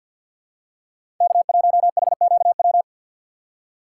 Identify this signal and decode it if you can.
Morse code, heard as a keyed tone, about 47 wpm, 700 Hz. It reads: K1HQW
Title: USA Call Signs - 4x Character Spacing - 50wpm